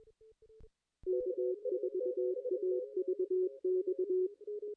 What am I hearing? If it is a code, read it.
Morse code, heard as a keyed tone, about 22 wpm, 370 Hz, with radio noise. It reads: K4AVX
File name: contests / ss-cw-k5tr-op-n5rz -2300